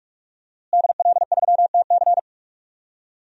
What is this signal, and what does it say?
Morse code, heard as a keyed tone, about 46 wpm, 700 Hz. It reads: DL3TC
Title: DXCC-100 Call Signs - 5 Characters - Repeated 3x - 45wpm